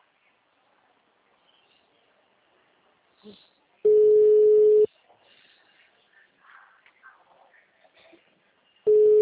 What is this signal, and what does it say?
Ringback tone (Europe, 425 Hz), with speech.